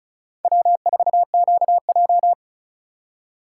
Morse code, heard as a keyed tone, about 36 wpm, 700 Hz. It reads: W4QJ
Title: USA Call Signs - 4 Characters - 2x Character Spacing - 35wpm